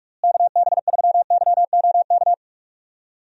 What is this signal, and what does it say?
Morse code, heard as a keyed tone, about 46 wpm, 700 Hz. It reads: KB3YOK
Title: USA Call Signs - 6 Characters - Repeated 3x - 45wpm